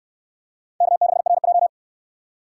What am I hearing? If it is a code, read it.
Morse code, heard as a keyed tone, about 69 wpm, 700 Hz. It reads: K6RJ